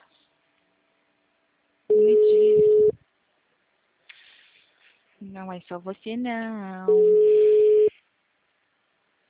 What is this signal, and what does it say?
Ringback tone (Europe, 425 Hz), with speech.
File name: out- -2011- - - .246